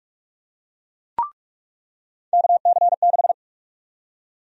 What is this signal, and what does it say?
Morse code, heard as a keyed tone, about 44 wpm, 700 Hz. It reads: KC6
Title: USA Call Sign Prefixes - 45wpm